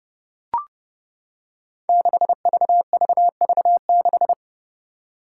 Morse code, heard as a keyed tone, about 30 wpm, 700 Hz. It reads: 6VVV6
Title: Five Challenging Letter-Number Combinations 30wpm